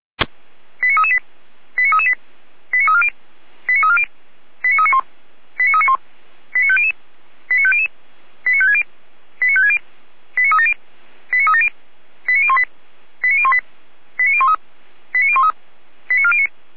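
Five-tone selective call (ZVEI), heard as quick runs of five tones, tones of about 70 ms. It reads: 89338 89338 89340 89340 89391 89391 89500 89500 89560 89560 89370 89370 89018 89018 89013 89013 89409
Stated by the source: ZVEI five-tone call received by SDR, fire brigade Orlamünde (feuerwehr-orlamuende).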